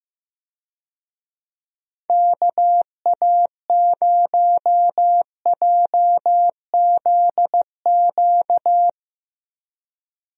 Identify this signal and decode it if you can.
Morse code, heard as a keyed tone, about 15 wpm, 700 Hz. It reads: KA0JZQ
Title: USA Call Signs - 6 Characters - Repeated 3x - 15wpm